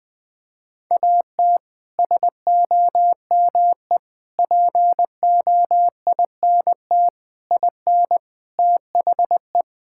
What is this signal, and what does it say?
Morse code, heard as a keyed tone, about 21 wpm, 700 Hz. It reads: AT SOME POINT IN THE
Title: Sets of 6 Words - 20wpm